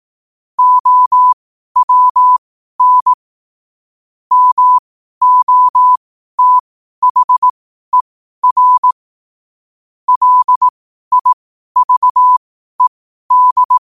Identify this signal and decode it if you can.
Morse code, heard as a keyed tone, about 18 wpm, spaced as for 12 wpm, 1000 Hz. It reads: OWN MOTHER LIVED